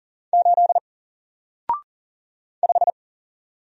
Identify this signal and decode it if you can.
Morse code, heard as a keyed tone, about 40 wpm, 700 Hz. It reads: 8 5
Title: Single Letter-Number Alt 40wpm